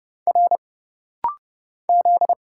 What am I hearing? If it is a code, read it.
Morse code, heard as a keyed tone, about 30 wpm, 700 Hz. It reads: R Z
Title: Single Letter-Number Alt - Rapid-Fire 30wpm